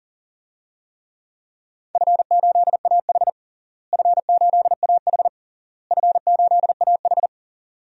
Morse code, heard as a keyed tone, about 40 wpm, 700 Hz. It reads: F8AH F8AH F8AH
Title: DXCC-100 Call Signs - 4 Characters - Repeated 3x - 40wpm